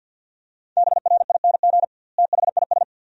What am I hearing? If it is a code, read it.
Morse code, heard as a keyed tone, about 51 wpm, 700 Hz. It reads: BRING THIS
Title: 2x Word-Spacing - Sentences from Top 500 Words - 50wpm